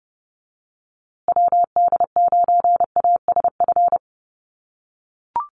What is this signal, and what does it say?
Morse code, heard as a keyed tone, about 30 wpm, 700 Hz. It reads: WD9ASF
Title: USA Call Signs Repeated 2x 30wpm